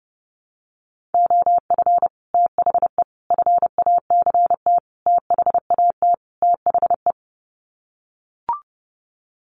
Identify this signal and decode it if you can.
Morse code, heard as a keyed tone, about 30 wpm, 700 Hz. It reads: OF THE FACT THAT THE
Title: Sets of 5 Words Encore2 30wpm